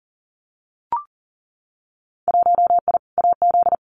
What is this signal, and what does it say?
Morse code, heard as a keyed tone, about 41 wpm, 700 Hz. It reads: 1I AZ
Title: Winter Field Day Exchanges 40wpm